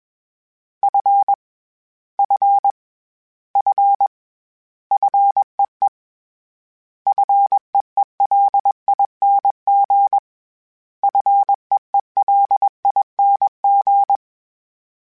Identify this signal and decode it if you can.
Morse code, heard as a keyed tone, about 21 wpm, 800 Hz. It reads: F F F FEE FEELING FEELING
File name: Short Runs / Short Runs-mp3s / SinglePrefixes @20